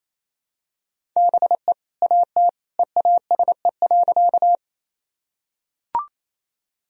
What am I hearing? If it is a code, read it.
Morse code, heard as a keyed tone, about 28 wpm, 700 Hz. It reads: BE AT EASE.
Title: Sentences from Top 1000 Words - Review Part 1 - 28wpm